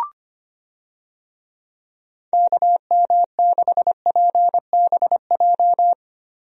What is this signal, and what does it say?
Morse code, heard as a keyed tone, about 25 wpm, 700 Hz. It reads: KM6PBJ
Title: USA Call Signs - 6 Characters - Repeated 2x - 25wpm